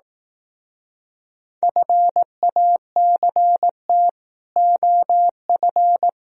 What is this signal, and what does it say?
Morse code, heard as a keyed tone, about 18 wpm, 700 Hz. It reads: FACT OF
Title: Speed-Racing - Sets of 5 Words 15wpm